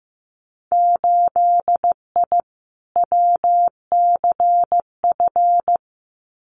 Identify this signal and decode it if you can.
Morse code, heard as a keyed tone, about 15 wpm, 700 Hz. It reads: 8I WCF
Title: Winter Field Day Exchanges Encore 15wpm